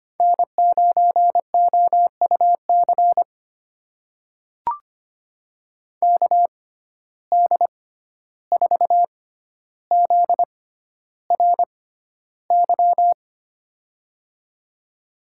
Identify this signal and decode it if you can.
Morse code, heard as a keyed tone, about 25 wpm, spaced as for 12 wpm, 700 Hz. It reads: N9OUC KD4ZRY
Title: USA Call Signs - 6x Character Spacing - 25wpm